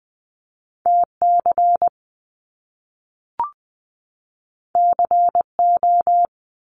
Morse code, heard as a keyed tone, about 20 wpm, 700 Hz. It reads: T C C O
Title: Top Two Letter Combinations in Words 20wpm